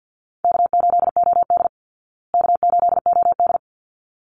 Morse code, heard as a keyed tone, about 50 wpm, 700 Hz. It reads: K8OD K8OD
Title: Call Signs Repeated 3x - 50wpm